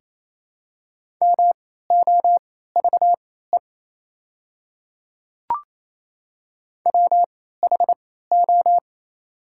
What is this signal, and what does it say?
Morse code, heard as a keyed tone, about 28 wpm, spaced as for 16 wpm, 700 Hz. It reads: MOVE WHO
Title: 3x Character Spacing - Top 200 Words - 28wpm